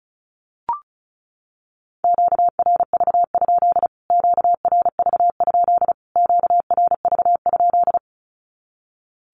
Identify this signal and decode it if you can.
Morse code, heard as a keyed tone, about 34 wpm, 700 Hz. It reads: QRV? QRV? QRV?
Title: Q-Codes Repeated 3x 35wpm